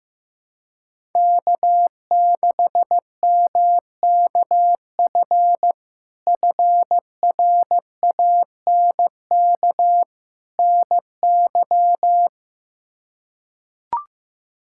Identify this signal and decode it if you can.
Morse code, heard as a keyed tone, about 15 wpm, 700 Hz. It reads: K6MKF FRANK NY
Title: K1USN SST QSO Practice - 15wpm